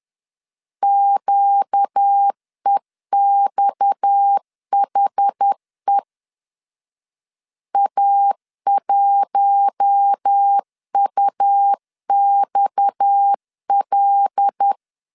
Morse code, heard as a keyed tone, about 10 wpm, 800 Hz. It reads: QEXHE A1UXL
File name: LEZIONI / 6 PKX0 / 6 PKX 10wpm 2426